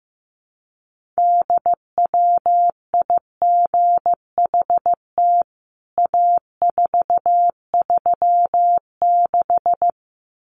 Morse code, heard as a keyed tone, about 15 wpm, 700 Hz. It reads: DWIGHT A436
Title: CWT Exchange 15wpm